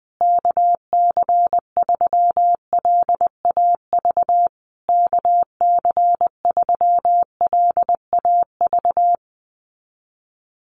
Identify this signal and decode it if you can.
Morse code, heard as a keyed tone, about 20 wpm, 700 Hz. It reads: KC3LAV KC3LAV
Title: USA Call Signs Repeated 3x 20wpm